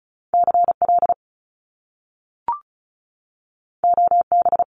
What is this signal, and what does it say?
Morse code, heard as a keyed tone, about 35 wpm, 700 Hz. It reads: CL OB
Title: Top Two Letter Combinations in Words 35wpm